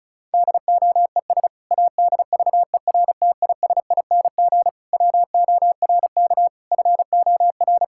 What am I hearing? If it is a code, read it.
Morse code, heard as a keyed tone, about 36 wpm, 690 Hz. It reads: DOES ADVERTISING WORK FOR
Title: Sentences from Top 1000 Words - Review Part 6 - 35wpm